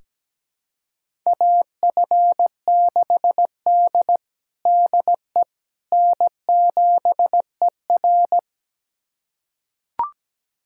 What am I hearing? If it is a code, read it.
Morse code, heard as a keyed tone, about 17 wpm, 700 Hz. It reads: AF6D DE N7ER
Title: USA Callsign DE Callsign - 4 Characters - 17wpm